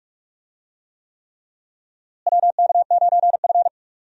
Morse code, heard as a keyed tone, about 46 wpm, 700 Hz. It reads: WK9F
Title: USA Call Signs - 4 Characters - Repeated 3x - 45wpm